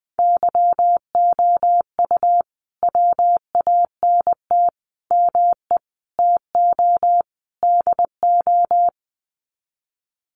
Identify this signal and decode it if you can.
Morse code, heard as a keyed tone, about 20 wpm, 700 Hz. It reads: YOU WANT ME TO DO
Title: Sets of 5 Words Encore2 20wpm